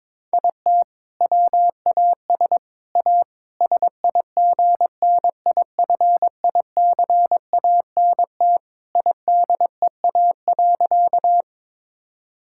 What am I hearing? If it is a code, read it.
Morse code, heard as a keyed tone, about 22 wpm, 700 Hz. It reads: IT WAS A SIGNIFICANT IDEA.
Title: Sentences from Top 1000 Words - Review Part 6 - 22wpm